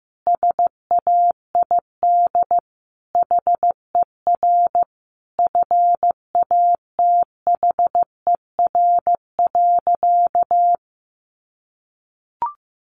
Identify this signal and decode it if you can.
Morse code, heard as a keyed tone, about 15 wpm, 700 Hz. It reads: SAID HER FATHER.
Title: Sentences from Top 300 Words 15wpm